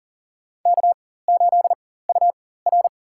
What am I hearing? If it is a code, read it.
Morse code, heard as a keyed tone, about 40 wpm, spaced as for 19 wpm, 700 Hz. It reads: K8UR
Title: USA Call Signs - 4 Characters - 4x Character Spacing - 40wpm